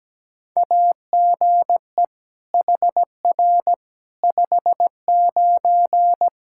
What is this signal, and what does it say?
Morse code, heard as a keyed tone, about 17 wpm, 700 Hz. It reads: AGE HR 59
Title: Practice QSO - Exchange 3 - More Info - 17wpm